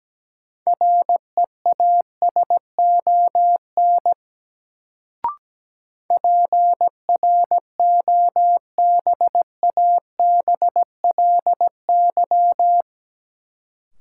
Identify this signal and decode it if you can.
Morse code, heard as a keyed tone, about 17 wpm, 700 Hz. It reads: REASON PROBABLY
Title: Sentences from Top 700 Words - New Word Drill - 17wpm